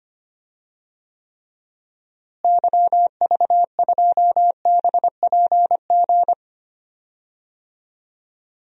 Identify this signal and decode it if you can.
Morse code, heard as a keyed tone, about 25 wpm, 700 Hz. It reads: YV2BPG
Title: DXCC-100 Call Signs - 6 Characters - Repeated 2x - 25wpm